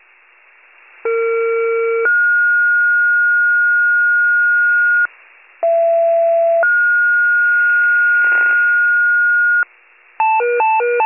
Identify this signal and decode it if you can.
Two-tone page, heard as a one-second tone, then a three-second tone, about 460 then 1430 Hz, about 670 then 1430 Hz. It is heard from 1.0 to 9.6 s.